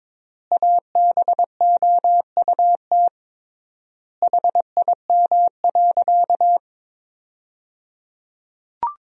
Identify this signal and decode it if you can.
Morse code, heard as a keyed tone, about 22 wpm, 700 Hz. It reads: ABOUT HIM.
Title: 3x Word-Spacing - Sentences from Top 200 Words - 22wpm